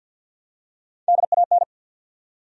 Morse code, heard as a keyed tone, about 52 wpm, 700 Hz. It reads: DAN
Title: USA Common Names - 50wpm